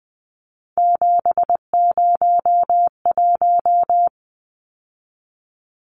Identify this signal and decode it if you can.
Morse code, heard as a keyed tone, about 20 wpm, 700 Hz. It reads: 701
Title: Numbers - 3-Digit 20wpm